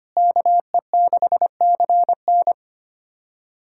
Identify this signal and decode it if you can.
Morse code, heard as a keyed tone, about 25 wpm, 700 Hz. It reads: KE6CN
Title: USA Callsign DE Callsign - 5 Characters - 25wpm